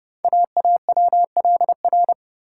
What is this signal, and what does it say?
Morse code, heard as a keyed tone, about 30 wpm, 700 Hz. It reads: AAWLR
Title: Code Groups - 5 Letters - 30wpm